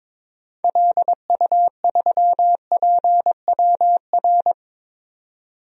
Morse code, heard as a keyed tone, about 22 wpm, 700 Hz. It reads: LU3PWR
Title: DXCC-100 Call Signs - 6 Characters - 22wpm